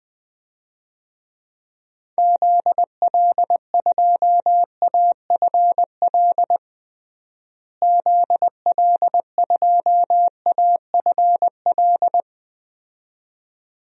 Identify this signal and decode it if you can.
Morse code, heard as a keyed tone, about 20 wpm, 700 Hz. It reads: ZL2AFL ZL2AFL
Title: DXCC-100 Call Signs - 6 Characters - Repeated 2x - 20wpm